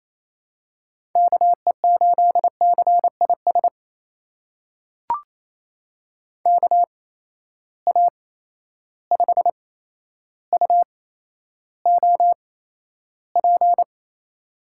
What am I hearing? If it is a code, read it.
Morse code, heard as a keyed tone, about 28 wpm, spaced as for 12 wpm, 700 Hz. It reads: KE8CIS KA5UOP
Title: USA Call Signs - 6 Characters - 8x Character Spacing - 28wpm